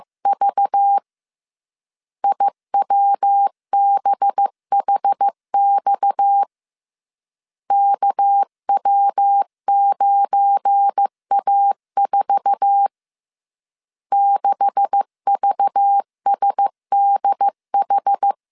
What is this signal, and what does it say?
Morse code, heard as a keyed tone, about 15 wpm, 800 Hz. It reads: V IWBHX KW9A4 6VSDH
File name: CW / DET / 1 9 1500C S A 14wpm 3222